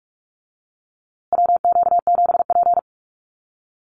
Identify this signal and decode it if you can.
Morse code, heard as a keyed tone, about 44 wpm, 700 Hz. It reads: WQ7P
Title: USA Call Signs Repeated 2x 45wpm